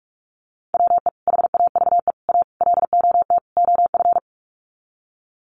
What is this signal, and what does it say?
Morse code, heard as a keyed tone, about 44 wpm, 700 Hz. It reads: WE HAVE A LOT OF
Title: Sets of 5 Words 45wpm